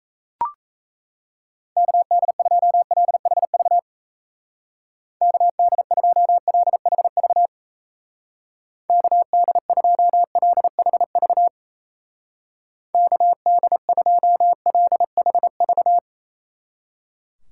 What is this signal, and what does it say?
Morse code, heard as a keyed tone, about 37 wpm, 700 Hz. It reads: KD2LHV KD2LHV KD2LHV KD2LHV